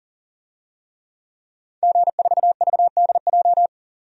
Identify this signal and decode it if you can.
Morse code, heard as a keyed tone, about 40 wpm, 700 Hz. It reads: G4VDJ
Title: DXCC-100 Call Signs - 5 Characters - 40wpm